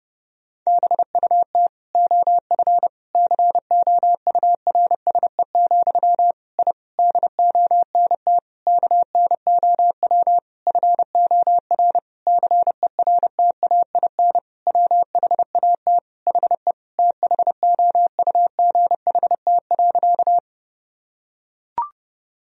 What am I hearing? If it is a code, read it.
Morse code, heard as a keyed tone, about 30 wpm, 700 Hz. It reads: BUT OF COURSE, I DONT KNOW FOR CERTAIN WHAT HE THOUGHT.
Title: Sentences from Top 1000 Words - Review Part 2 - 30wpm